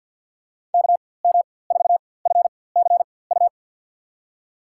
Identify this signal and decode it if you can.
Morse code, heard as a keyed tone, about 54 wpm, spaced as for 24 wpm, 700 Hz. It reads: KM4FCU
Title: USA Call Signs - 4x Character Spacing - 50wpm